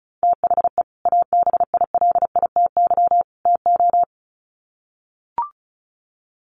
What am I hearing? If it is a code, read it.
Morse code, heard as a keyed tone, about 34 wpm, 700 Hz. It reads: THE ABILITY TO
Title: Sets of 3 Words 35wpm